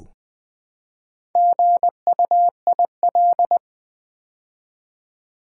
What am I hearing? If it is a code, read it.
Morse code, heard as a keyed tone, about 20 wpm, 700 Hz. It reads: GUIL
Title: Build A Word - 2 Letters At A Time - 6-Letter Words - 20wpm